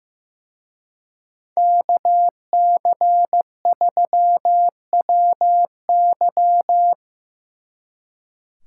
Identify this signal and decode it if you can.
Morse code, heard as a keyed tone, about 16 wpm, 700 Hz. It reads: KC3WY
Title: Speed Racing - USA 2x2 Call Signs 15wpm